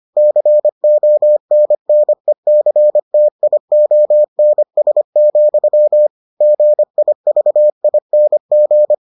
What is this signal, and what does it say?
Morse code, heard as a keyed tone, about 25 wpm, 600 Hz. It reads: CONNECTIONS, GIVING